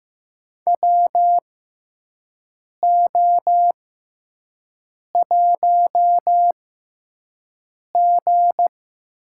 Morse code, heard as a keyed tone, about 15 wpm, spaced as for 5 wpm, 700 Hz. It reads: WO1G
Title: USA Call Signs - 4 Characters - 6x Character Spacing - 15wpm